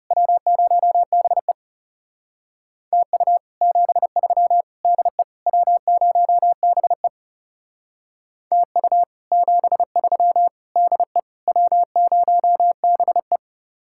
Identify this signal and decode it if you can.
Morse code, heard as a keyed tone, about 37 wpm, 700 Hz. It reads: W0BE TU 73 DE W0BE TU 73 DE W0BE
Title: Speed-Racing Common QSO Phrases - 30wpm